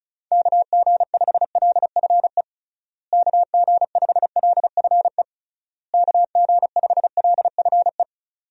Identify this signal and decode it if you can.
Morse code, heard as a keyed tone, about 36 wpm, 700 Hz. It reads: KG5LFE KG5LFE KG5LFE
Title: USA Call Signs - 6 Characters - Repeated 3x - 35wpm